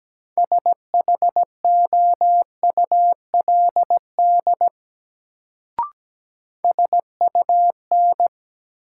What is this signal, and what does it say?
Morse code, heard as a keyed tone, about 17 wpm, 700 Hz. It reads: SHOULD SUN
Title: Sentences from Top 300 Words - New Word Drill - 17wpm